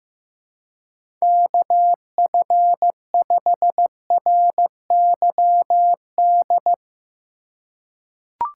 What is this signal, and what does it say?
Morse code, heard as a keyed tone, about 15 wpm, 700 Hz. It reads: KF5RYD
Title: USA Call Signs - 6 Characters - 3x Character Spacing - 15wpm